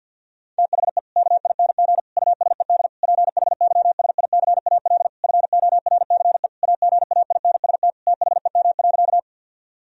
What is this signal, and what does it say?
Morse code, heard as a keyed tone, about 50 wpm, 700 Hz. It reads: THE KING USED PHYSICAL FORCE AGAINST THEM.
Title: Sentences from Top 1000 Words - Review Part 4 - 50wpm